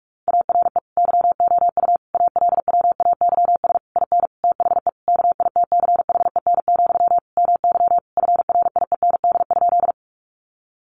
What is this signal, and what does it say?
Morse code, heard as a keyed tone, about 44 wpm, 700 Hz. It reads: ARE YOU ALWAYS IN THE KITCHEN, MY FRIEND?
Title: Sentences from Top 1400 Words 45wpm